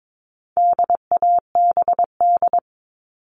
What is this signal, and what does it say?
Morse code, heard as a keyed tone, about 22 wpm, 700 Hz. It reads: DABD
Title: All Cut Numbers - 4-Digit 22wpm